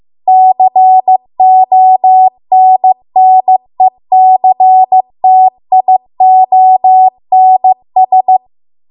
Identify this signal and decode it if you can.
Morse code, heard as a keyed tone, about 15 wpm, 750 Hz. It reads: CONNECTIONS